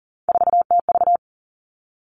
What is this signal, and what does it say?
Morse code, heard as a keyed tone, about 42 wpm, 700 Hz. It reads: 4TV
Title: Three Challenging Letter-Number Combinations 40wpm